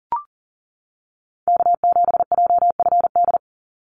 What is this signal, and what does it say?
Morse code, heard as a keyed tone, about 40 wpm, 700 Hz. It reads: K7JFD
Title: NA SOTA Activator Callsigns 40wpm